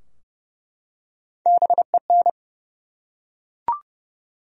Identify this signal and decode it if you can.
Morse code, heard as a keyed tone, about 30 wpm, 700 Hz. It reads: BEN
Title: USA Common Names - Max 3 Letters - 30wpm